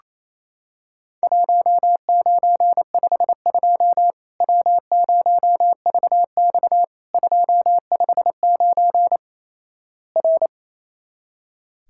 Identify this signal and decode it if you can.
Morse code, heard as a keyed tone, about 28 wpm, 700 Hz. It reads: 1952 W0VX 259 R
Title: Worked All Europe DX Contest - QTC Exchange - 28wpm